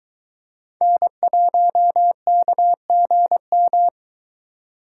Morse code, heard as a keyed tone, about 22 wpm, 700 Hz. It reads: N1KGM